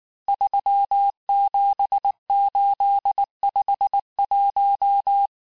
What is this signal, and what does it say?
Morse code, heard as a keyed tone, about 19 wpm, 800 Hz. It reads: 37851
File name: cours-du-mercredi / audios / cours-869-texte-CHIFFRE-vitesse-18-MPM